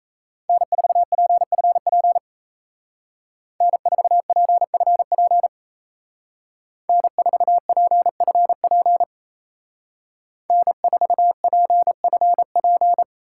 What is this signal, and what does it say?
Morse code, heard as a keyed tone, about 33 wpm, 700 Hz. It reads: N4PFP N4PFP N4PFP N4PFP
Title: Speed Racing - USA Call Signs 28wpm